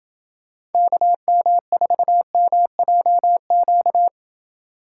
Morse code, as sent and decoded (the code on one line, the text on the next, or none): -.- -- ....- -- .--- --.-
KM4MJQ